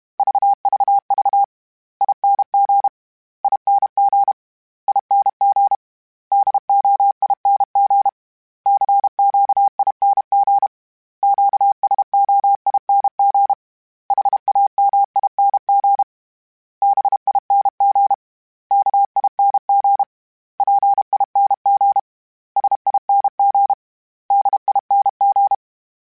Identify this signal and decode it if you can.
Morse code, heard as a keyed tone, about 32 wpm, 800 Hz. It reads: VVV ING ING ING DOING CQING QSOING HAMING BING KING PING SING DING
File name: Special INGs-mp3s / Special INGs-mp3s / Sngl Ing @30